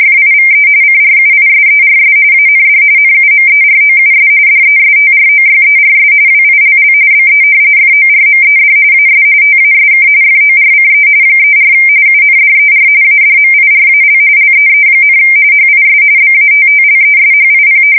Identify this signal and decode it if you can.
RTTY, about 45.45 baud. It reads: FIVE CRUISERS AND NUMEROUS DESTROYERS WILL BE FORMED AS TASK FORCE 34 UNDER ADMIRAL LEE, HIS BATTLESHIP
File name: fsk170 / battleshipnj